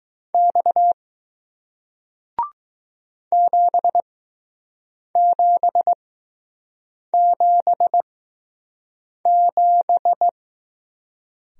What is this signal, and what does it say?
Morse code, heard as a keyed tone, about 20 wpm, spaced as for 6 wpm, 700 Hz. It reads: X 7777